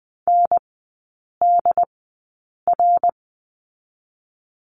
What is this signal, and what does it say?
Morse code, heard as a keyed tone, about 19 wpm, spaced as for 8 wpm, 700 Hz. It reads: NDR
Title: 067 Lesson 12e - Letter d - Three Characters 20wpm